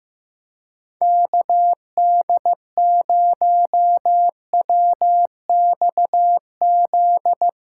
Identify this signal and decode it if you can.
Morse code, heard as a keyed tone, about 15 wpm, 700 Hz. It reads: KD0WXZ
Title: USA Call Signs - 6 Characters - 2x Character Spacing - 15wpm